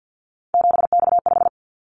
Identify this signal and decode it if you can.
Morse code, heard as a keyed tone, about 50 wpm, 700 Hz. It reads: 7X5
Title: Callsign Prefixes for Top 100 DXCC - 50wpm